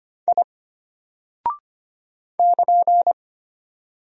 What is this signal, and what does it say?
Morse code, heard as a keyed tone, about 25 wpm, 700 Hz. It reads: I <KN>
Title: Single Character (Letter:Number:Punctuation:Prosign) Alt 25wpm